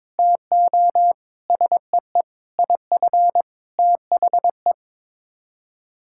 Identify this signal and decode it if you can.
Morse code, heard as a keyed tone, about 22 wpm, 700 Hz. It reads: TO SEE IF THE
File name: practice / 22wpm / Sets of 8 Words - 22wpm